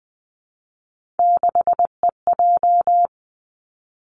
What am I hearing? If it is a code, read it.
Morse code, heard as a keyed tone, about 20 wpm, 700 Hz. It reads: 6EJ
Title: Three Challenging Letter-Number Combinations 20wpm